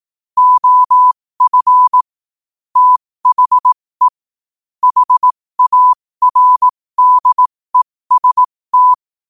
Morse code, heard as a keyed tone, about 18 wpm, spaced as for 15 wpm, 1000 Hz. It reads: OF THE HARDEST
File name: qotdfiles / 15wpm